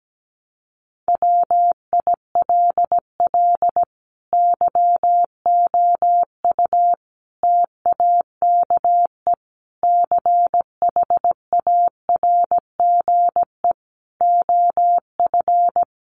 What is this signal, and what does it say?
Morse code, heard as a keyed tone, about 17 wpm, 700 Hz. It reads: WILL YOU TAKE CHARGE OF
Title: Sentences from Top 900 Words 17wpm